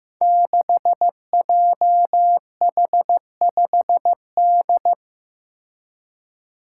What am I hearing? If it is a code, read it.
Morse code, heard as a keyed tone, about 15 wpm, 700 Hz. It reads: 6JH5D
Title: Five Challenging Letter-Number Combinations 15wpm